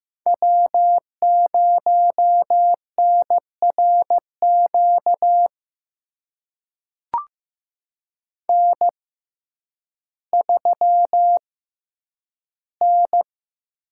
Morse code, heard as a keyed tone, about 15 wpm, spaced as for 9 wpm, 700 Hz. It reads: W0NRQ N3N
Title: USA Call Signs - 5 Characters - 6x Character Spacing - 15wpm